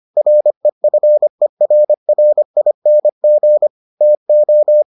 Morse code, heard as a keyed tone, about 25 wpm, 600 Hz. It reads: REFERRING TO